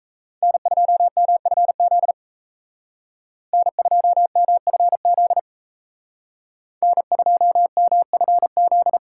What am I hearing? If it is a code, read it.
Morse code, heard as a keyed tone, about 34 wpm, 700 Hz. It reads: N2MFZ N2MFZ N2MFZ
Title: Speed Racing - USA 1x3 Call Signs 28wpm